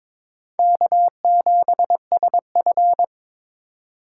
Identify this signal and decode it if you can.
Morse code, heard as a keyed tone, about 21 wpm, 700 Hz. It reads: K7SF
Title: K1USN SST QSO Practice - 22wpm